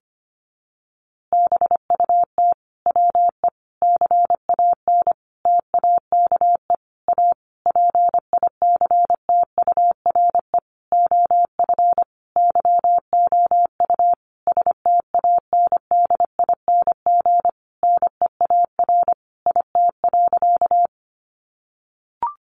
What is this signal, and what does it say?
Morse code, heard as a keyed tone, about 25 wpm, 700 Hz. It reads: BUT WE CAN TAKE A PICTURE OF YOU STANDING NEAR IT.
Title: A Sound Of Thunder 25wpm